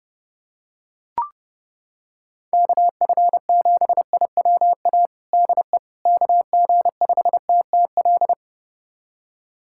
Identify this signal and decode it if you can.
Morse code, heard as a keyed tone, about 30 wpm, 700 Hz. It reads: KF7IWA DE KG5TTL
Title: USA Callsign DE Callsign 30wpm